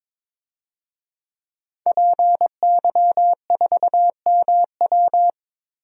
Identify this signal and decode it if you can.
Morse code, heard as a keyed tone, about 22 wpm, 700 Hz. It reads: PY4MW
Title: DXCC-100 Call Signs - 5 Characters - Repeated 3x - 22wpm